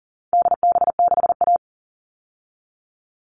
Morse code, heard as a keyed tone, about 40 wpm, 700 Hz. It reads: DB6A
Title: All Cut Numbers - 4-Digit 40wpm